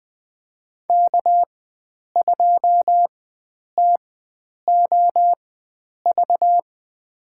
Morse code, heard as a keyed tone, about 20 wpm, spaced as for 9 wpm, 700 Hz. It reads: K2TOV
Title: USA Call Signs - 5 Characters - 4x Character Spacing - 20wpm